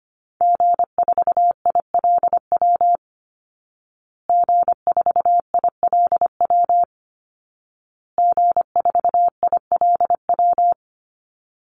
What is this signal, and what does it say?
Morse code, heard as a keyed tone, about 25 wpm, 700 Hz. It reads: G4ILW G4ILW G4ILW
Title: Call Signs Repeated 2x - 25wpm